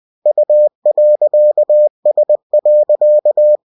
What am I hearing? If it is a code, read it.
Morse code, heard as a keyed tone, about 20 wpm, 600 Hz. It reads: U.S.